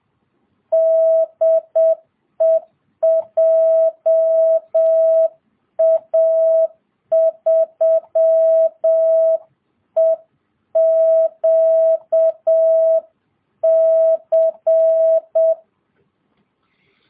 Morse code, heard as a keyed tone, about 7 wpm, 660 Hz. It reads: DE JA3EQC